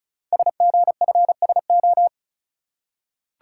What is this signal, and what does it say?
Morse code, heard as a keyed tone, about 36 wpm, 700 Hz. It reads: SGFSO
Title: Code Groups - 5 Letters - 35wpm